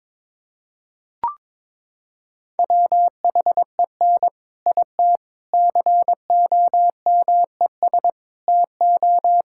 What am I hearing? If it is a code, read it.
Morse code, heard as a keyed tone, about 22 wpm, 700 Hz. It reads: WHEN IT COMES TO
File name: practice / 22wpm / Sentences with 3-Word Phrases - 22wpm